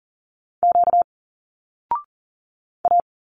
Morse code, heard as a keyed tone, about 40 wpm, 700 Hz. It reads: Q A
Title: Single Letters 40wpm